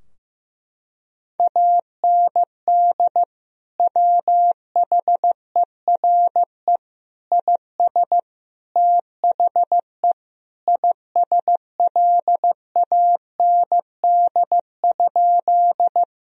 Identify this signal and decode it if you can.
Morse code, heard as a keyed tone, about 15 wpm, 700 Hz. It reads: AND WHERE IS THE ISLAND?
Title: Sentences from Top 1000 Words - Review Part 3 - 15wpm